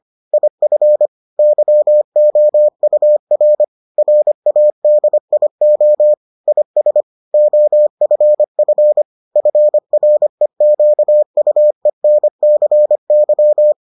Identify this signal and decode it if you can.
Morse code, heard as a keyed tone, about 25 wpm, 600 Hz. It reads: IF YOUR RADIO IS OFF FREQUENCY